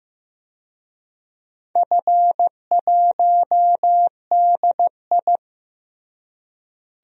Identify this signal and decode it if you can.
Morse code, heard as a keyed tone, about 15 wpm, 700 Hz. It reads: F1DI
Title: DXCC-100 Call Signs - 4 Characters - Repeated 3x - 15wpm